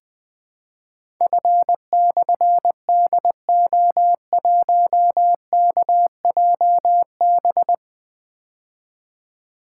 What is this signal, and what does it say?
Morse code, heard as a keyed tone, about 20 wpm, 700 Hz. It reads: F/DO1KJB
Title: Difficult Call Signs 20wpm